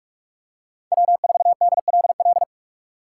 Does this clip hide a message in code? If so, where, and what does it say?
Yes: Morse code from 0.9 to 2.4 s, heard as a keyed tone, about 44 wpm, 700 Hz. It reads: W4DLL